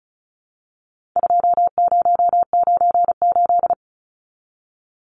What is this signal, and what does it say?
Morse code, heard as a keyed tone, about 34 wpm, 700 Hz. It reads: 2098
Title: Numbers - 4-Digit 35wpm